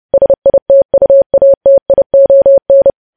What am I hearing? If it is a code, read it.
Morse code, heard as a keyed tone, about 30 wpm, 550 Hz. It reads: SITUATION